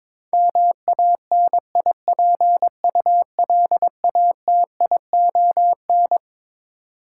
Morse code, heard as a keyed tone, about 22 wpm, 700 Hz. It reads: MANIPULATION
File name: Twelve Letter Words - 22wpm